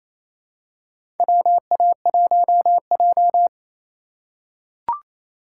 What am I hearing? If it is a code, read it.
Morse code, heard as a keyed tone, about 28 wpm, 700 Hz. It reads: WA1J
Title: USA Call Signs - 4 Characters - 4x Character Spacing - 28wpm